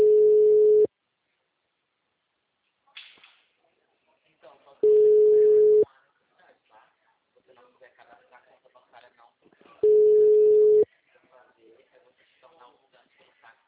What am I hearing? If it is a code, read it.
Ringback tone (Europe, 425 Hz), with speech.